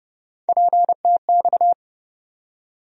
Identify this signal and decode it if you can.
Morse code, heard as a keyed tone, about 30 wpm, 700 Hz. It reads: PTX
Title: USA Call Sign Postfixes - 3 Letters 30wpm